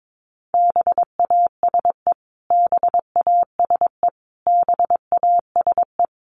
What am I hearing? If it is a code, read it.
Morse code, heard as a keyed tone, about 22 wpm, 700 Hz. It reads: BASE BASE BASE
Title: Top 300 Words Repeated 22wpm